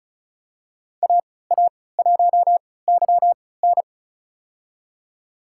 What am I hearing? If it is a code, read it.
Morse code, heard as a keyed tone, about 34 wpm, spaced as for 20 wpm, 700 Hz. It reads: AA1YN
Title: USA Call Signs - 5 Characters - 3x Character Spacing - 35wpm